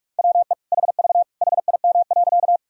Morse code, heard as a keyed tone, about 46 wpm, 700 Hz. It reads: WE HV HIM.
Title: Sentences from Top 100 Words and Abbreviations 45wpm